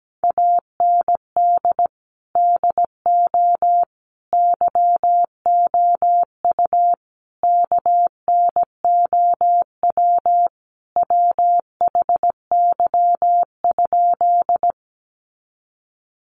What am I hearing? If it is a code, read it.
Morse code, heard as a keyed tone, about 17 wpm, 700 Hz. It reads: AND DO YOU KNOW WHY?
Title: Sentences from Top 200 Words 17wpm